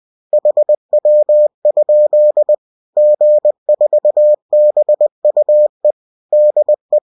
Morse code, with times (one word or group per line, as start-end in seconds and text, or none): 0.3-2.5 HW?
3.0-5.9 G4BUE
6.3-7.0 DE